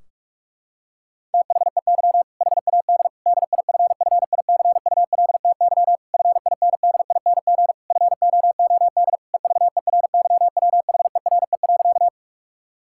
Morse code, heard as a keyed tone, about 41 wpm, 700 Hz. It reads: THEY HAD DIFFICULTY FINDING FOOD EVERYWHERE.
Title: Sentences from Top 1600 Words 45wpm